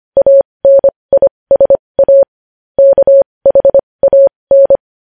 Morse code, heard as a keyed tone, about 25 wpm, spaced as for 20 wpm, 550 Hz. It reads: ANISA KHAN
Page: News Headlines in Morse Code at 20 WPM